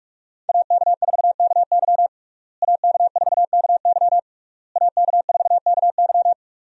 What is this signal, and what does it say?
Morse code, heard as a keyed tone, about 41 wpm, 700 Hz. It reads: AK4KY AK4KY AK4KY
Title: USA Call Signs - 5 Characters - Repeated 3x - 45wpm